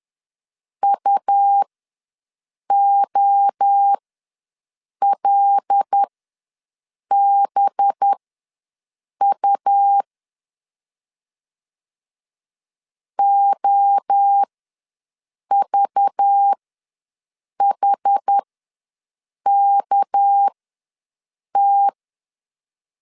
Morse code, heard as a keyed tone, about 10 wpm, spaced as for 6 wpm, 800 Hz. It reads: UOLBU OVHKT
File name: LEZIONI / 7 GF9Z / 7 GF ws 10wpm 3918